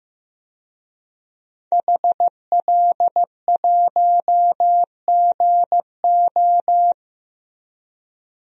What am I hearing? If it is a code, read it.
Morse code, heard as a keyed tone, about 15 wpm, 700 Hz. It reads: HL1GO
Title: DXCC-100 Call Signs - 5 Characters - 15wpm